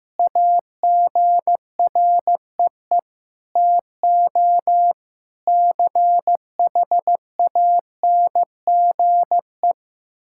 Morse code, heard as a keyed tone, about 15 wpm, 700 Hz. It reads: AGREE TO CHANGE
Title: Sentences from Top 1000 Words - Review Part 6 - 15wpm